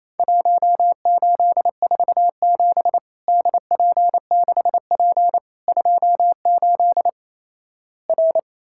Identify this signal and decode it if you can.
Morse code, heard as a keyed tone, about 28 wpm, 700 Hz. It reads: 1847 DP6P 28 R
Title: Worked All Europe DX Contest - QTC Exchange - 28wpm